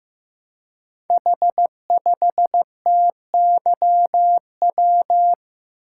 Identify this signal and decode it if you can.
Morse code, heard as a keyed tone, about 15 wpm, 700 Hz. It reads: H5TYW
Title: Code Groups - 5 Letter:Numbers - 15wpm